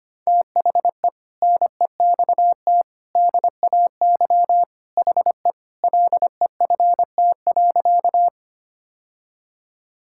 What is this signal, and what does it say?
Morse code, heard as a keyed tone, about 25 wpm, 700 Hz. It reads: THE NEXT DAY HE LEFT.
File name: practice / 25wpm / Sentences with 3-Word Phrases - 25wpm